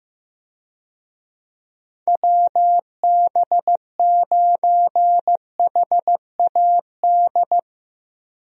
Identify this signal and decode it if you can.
Morse code, heard as a keyed tone, about 15 wpm, 700 Hz. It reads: WB9HAD
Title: USA Call Signs - 6 Characters - Repeated 3x - 15wpm